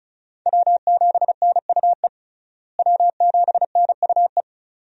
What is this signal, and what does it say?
Morse code, heard as a keyed tone, about 34 wpm, 700 Hz. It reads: W7NUE W7NUE
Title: USA Call Signs - 5 Characters - Repeated 2x - 35wpm